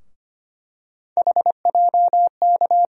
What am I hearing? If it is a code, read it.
Morse code, heard as a keyed tone, about 25 wpm, 700 Hz. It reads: HJK
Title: USA Call Sign Postfixes - 3 Letters 25wpm